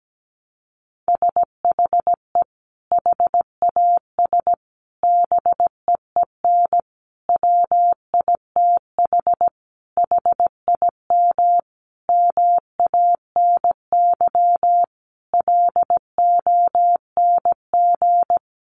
Morse code, heard as a keyed tone, about 17 wpm, 700 Hz. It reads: SHE HAS BEEN WITH HIM MANY LONG
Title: Sentences from Top 200 Words 17wpm